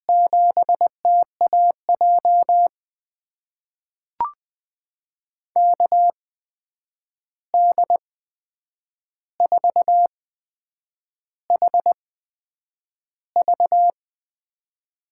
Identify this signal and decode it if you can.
Morse code, heard as a keyed tone, about 21 wpm, spaced as for 8 wpm, 700 Hz. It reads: 7TAJ KD4HV
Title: USA Call Signs - 5 Characters - 8x Character Spacing - 20wpm